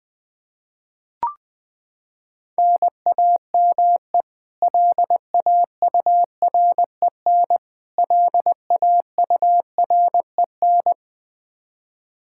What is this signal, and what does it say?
Morse code, heard as a keyed tone, about 20 wpm, 700 Hz. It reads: NAME LAUREN LAUREN
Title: Practice QSO - Exchange 1 - Beginning - 20wpm